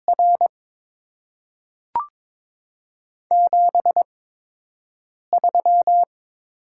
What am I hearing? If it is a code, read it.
Morse code, heard as a keyed tone, about 22 wpm, spaced as for 6 wpm, 700 Hz. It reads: R 73
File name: Top 20 QSO Elements - 8x Character Spacing - 22wpm